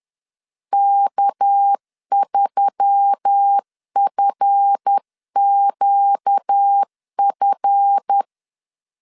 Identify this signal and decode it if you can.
Morse code, heard as a keyed tone, about 10 wpm, 800 Hz. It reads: K3FQF
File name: LEZIONI / 7 GF9Z / 7 GF 10wpm 2514